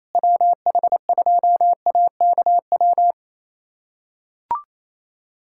Morse code, heard as a keyed tone, about 28 wpm, 700 Hz. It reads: WH2AKW
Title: USA Callsign DE Callsign - 6 Characters - 28wpm